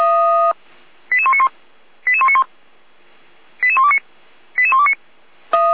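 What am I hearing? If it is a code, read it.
Five-tone selective call (ZVEI), heard as quick runs of five tones, tones of about 70 ms. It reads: 88191 88191 88129 88129
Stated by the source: ZVEI five-tone selective call received with an SDR.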